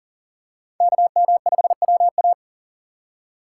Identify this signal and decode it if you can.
Morse code, heard as a keyed tone, about 40 wpm, 700 Hz. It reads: KM5WA
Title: USA Call Signs - 5 Characters - 6x Character Spacing - 40wpm